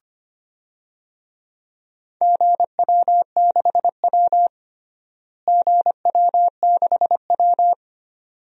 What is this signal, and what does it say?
Morse code, heard as a keyed tone, about 25 wpm, 700 Hz. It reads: GW6W GW6W
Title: DXCC-100 Call Signs - 4 Characters - Repeated 2x - 25wpm